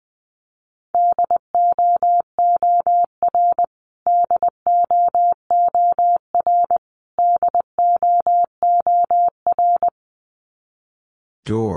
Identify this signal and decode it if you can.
Morse code, heard as a keyed tone, about 20 wpm, 700 Hz. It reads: DOOR DOOR DOOR
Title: Top 300 Words Repeated 20wpm